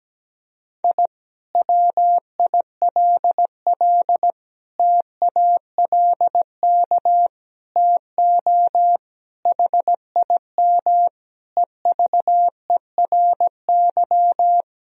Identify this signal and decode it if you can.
Morse code, heard as a keyed tone, about 17 wpm, 700 Hz. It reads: I WILL TALK TO HIM EVERY
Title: Sentences from Top 1000 Words - Review Part 2 - 17wpm